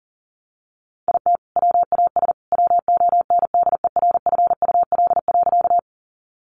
Morse code, heard as a keyed tone, about 38 wpm, 700 Hz. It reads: IT WAS WONDERFUL.
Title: Sentences from Top 1100 Words 40wpm